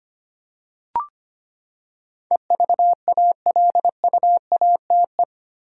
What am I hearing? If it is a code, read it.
Morse code, heard as a keyed tone, about 25 wpm, 700 Hz. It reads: EVALUATE
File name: Eight Letter Words - 25wpm